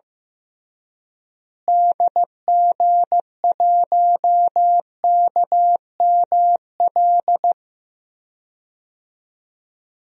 Morse code, heard as a keyed tone, about 15 wpm, 700 Hz. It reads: DG1KML
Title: DXCC-100 Call Signs - 6 Characters - Repeated 3x - 15wpm